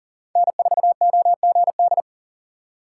Morse code, heard as a keyed tone, about 40 wpm, 700 Hz. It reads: N4OGD
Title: USA Call Signs - 5 Characters - 4x Character Spacing - 40wpm